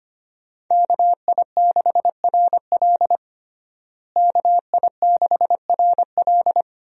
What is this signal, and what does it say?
Morse code, heard as a keyed tone, about 25 wpm, 700 Hz. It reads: KI6RL KI6RL
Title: USA Call Signs - 5 Characters - Repeated 2x - 25wpm